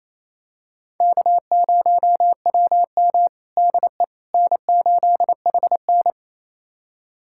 Morse code, heard as a keyed tone, about 28 wpm, 700 Hz. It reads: K0WM DE N8HN
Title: USA Callsign DE Callsign - 4 Characters - 28wpm